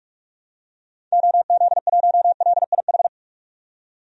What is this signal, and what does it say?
Morse code, heard as a keyed tone, about 42 wpm, 700 Hz. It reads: OZ1LIH
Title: DXCC-100 Call Signs - 6 Characters - 45wpm